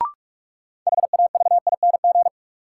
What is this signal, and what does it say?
Morse code, heard as a keyed tone, about 41 wpm, 700 Hz. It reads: HAVING
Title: Sets of 5 Words Encore - New Word Drill - 45wpm